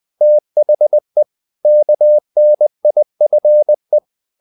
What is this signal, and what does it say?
Morse code, heard as a keyed tone, about 20 wpm, 600 Hz. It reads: THE KNIFE
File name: the old man and the sea part 2 20wpm